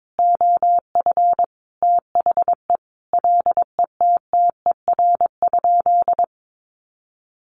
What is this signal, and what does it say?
Morse code, heard as a keyed tone, about 22 wpm, 700 Hz. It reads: OF THE LETTER?
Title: Sentences from Top 1300 Words 22wpm